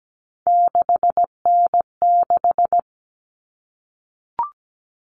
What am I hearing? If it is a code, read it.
Morse code, heard as a keyed tone, about 17 wpm, 700 Hz. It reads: 6N6
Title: All Cut Numbers - 3-Digit 17wpm